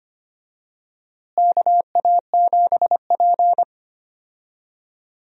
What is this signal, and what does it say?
Morse code, heard as a keyed tone, about 25 wpm, 700 Hz. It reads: KA7P
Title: USA Call Signs - 4 Characters - Repeated 3x - 25wpm